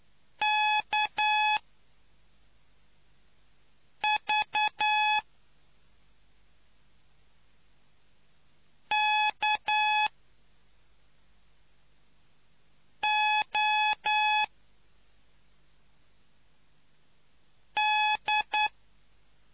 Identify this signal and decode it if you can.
Morse code, heard as a keyed tone, about 9 wpm, spaced as for 3 wpm, 850 Hz. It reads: KVKOD